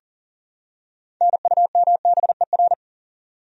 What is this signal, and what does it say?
Morse code, heard as a keyed tone, about 40 wpm, 700 Hz. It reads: NUMBER
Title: Top 100 Words 40wpm